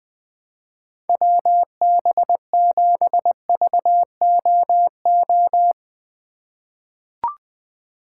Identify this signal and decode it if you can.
Morse code, heard as a keyed tone, about 20 wpm, 700 Hz. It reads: WB7VOO
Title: USA Call Signs - 4x Character Spacing - 20wpm